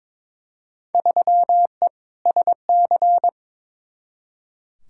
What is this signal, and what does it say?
Morse code, heard as a keyed tone, about 22 wpm, 700 Hz. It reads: 3E SC
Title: Field Day Exchanges Encore - 22wpm